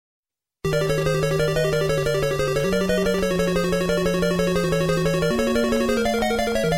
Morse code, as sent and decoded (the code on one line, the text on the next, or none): none